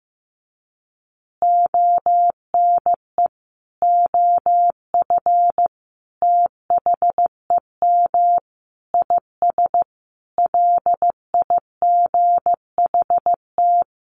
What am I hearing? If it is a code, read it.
Morse code, heard as a keyed tone, about 15 wpm, 700 Hz. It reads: ONE OF THEM IS LIGHT
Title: Additional Easy Phrases 15wpm